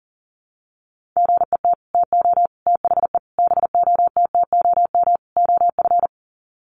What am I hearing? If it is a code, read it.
Morse code, heard as a keyed tone, about 40 wpm, 700 Hz. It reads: GET TO THE BOTTOM OF
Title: Sets of 5 Words Encore 40wpm